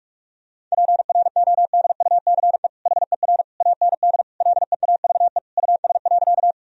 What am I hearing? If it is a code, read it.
Morse code, heard as a keyed tone, about 46 wpm, 700 Hz. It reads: PRODUCE HER AND LEAVE US.
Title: Sentences from Top 1000 Words - Review Part 3 - 45wpm